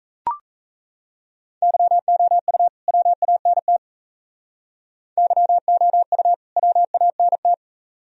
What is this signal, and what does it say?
Morse code, heard as a keyed tone, about 33 wpm, 700 Hz. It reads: YOU WANT YOU WANT
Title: Speed-Racing - Sets of 2 Words 28wpm